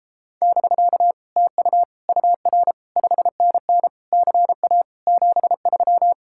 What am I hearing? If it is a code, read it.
Morse code, heard as a keyed tone, about 33 wpm, 700 Hz. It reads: <BK> TU UR 5NN CA 73